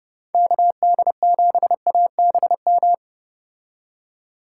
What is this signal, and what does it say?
Morse code, heard as a keyed tone, about 30 wpm, 700 Hz. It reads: KD7ABM